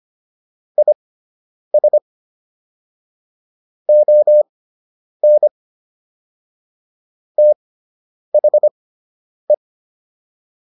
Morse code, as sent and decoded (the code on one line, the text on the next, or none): .. ... / --- -. / - .... .
IS ON THE